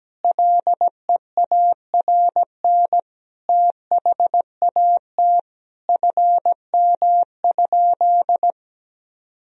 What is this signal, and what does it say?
Morse code, heard as a keyed tone, about 17 wpm, 700 Hz. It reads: LEARN THAT FM?
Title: Sentences from Top 300 Words and Abbreviations 17wpm